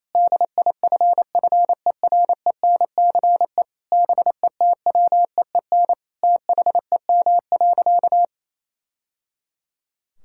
Morse code, heard as a keyed tone, about 28 wpm, 700 Hz. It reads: DIFFERENCE BETWEEN THEM.
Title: Sentences from Top 1000 Words - Review Part 5 - 28wpm